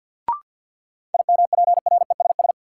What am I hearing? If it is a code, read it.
Morse code, heard as a keyed tone, about 53 wpm, 700 Hz. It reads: IMPRESS
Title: Sentences from Top 1600 Words - New Word Drill Alt - 50wpm